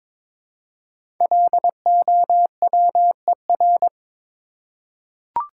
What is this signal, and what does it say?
Morse code, heard as a keyed tone, about 22 wpm, 690 Hz. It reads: LOWER
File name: 5 Letter Words - First 3 Letters Given - 22wpm